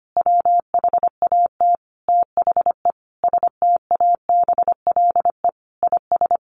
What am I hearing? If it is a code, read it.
Morse code, heard as a keyed tone, about 25 wpm, 700 Hz. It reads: WHAT THE STABLE IS
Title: Sentences from Top 1500 Words 25wpm